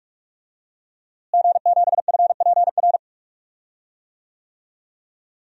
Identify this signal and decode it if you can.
Morse code, heard as a keyed tone, about 41 wpm, 700 Hz. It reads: G7FPR
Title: DXCC-100 Call Signs - 5 Characters - Repeated 2x - 45wpm